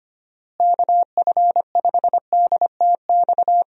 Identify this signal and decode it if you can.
Morse code, heard as a keyed tone, about 25 wpm, 700 Hz. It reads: KF5DTX
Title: USA Call Signs - 6 Characters - Repeated 3x - 25wpm